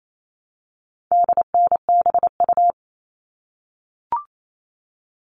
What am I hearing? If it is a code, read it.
Morse code, heard as a keyed tone, about 28 wpm, 700 Hz. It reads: DNBU
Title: All Cut Numbers - 4-Digit 28wpm